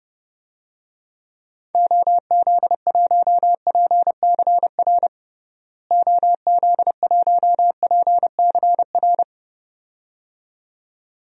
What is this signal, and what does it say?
Morse code, heard as a keyed tone, about 30 wpm, 700 Hz. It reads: OZ1PCR OZ1PCR
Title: DXCC-100 Call Signs - 6 Characters - Repeated 2x - 30wpm